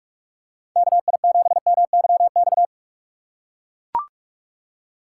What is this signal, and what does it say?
Morse code, heard as a keyed tone, about 44 wpm, 700 Hz. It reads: KI7MYX
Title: USA Call Signs - 6 Characters - 6x Character Spacing - 45wpm